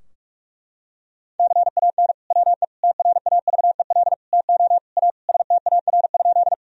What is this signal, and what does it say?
Morse code, heard as a keyed tone, about 47 wpm, 700 Hz. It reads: CAN WE TRAVEL TO A STAR?
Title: Sentences from Top 1000 Words - Review Part 2 - 45wpm